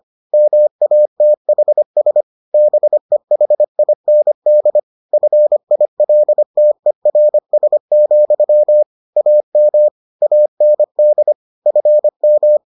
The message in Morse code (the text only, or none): MATHS BEHIND FILTERS, AM AND FM